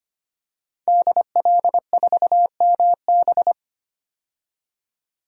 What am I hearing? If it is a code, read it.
Morse code, heard as a keyed tone, about 25 wpm, 700 Hz. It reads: DL4MB
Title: DXCC-100 Call Signs - 5 Characters - 25wpm